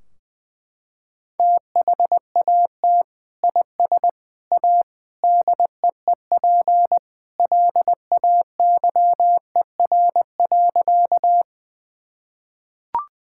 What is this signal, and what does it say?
Morse code, heard as a keyed tone, about 20 wpm, 700 Hz. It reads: THAT IS A DEEP LAYER.
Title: Sentences from Top 1600 Words 20wpm